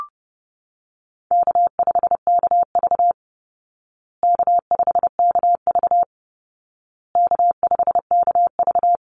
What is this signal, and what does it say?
Morse code, heard as a keyed tone, about 30 wpm, 700 Hz. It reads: K5KV K5KV K5KV
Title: Easy Call Signs Repeated 3x - 30wpm